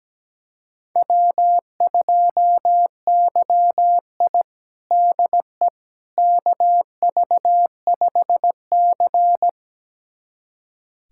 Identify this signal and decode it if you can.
Morse code, heard as a keyed tone, about 17 wpm, 700 Hz. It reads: W2YI DE KV5C
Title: USA Callsign DE Callsign - 4 Characters - 17wpm